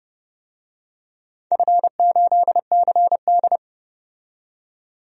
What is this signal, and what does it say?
Morse code, heard as a keyed tone, about 32 wpm, 700 Hz. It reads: F8CD
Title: DXCC-100 Call Signs - 4 Characters - 30wpm